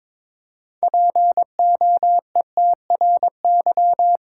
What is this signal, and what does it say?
Morse code, heard as a keyed tone, about 22 wpm, 700 Hz. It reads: POETRY
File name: Difficult Words with Hints - 22wpm